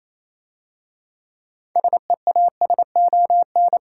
Morse code, heard as a keyed tone, about 27 wpm, 700 Hz. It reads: SEASON
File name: Easy Words - Speed-Racing with Hints - 22wpm